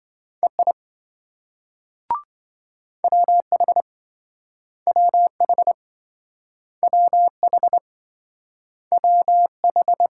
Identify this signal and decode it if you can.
Morse code, heard as a keyed tone, about 26 wpm, 700 Hz. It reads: EI WH WH WH WH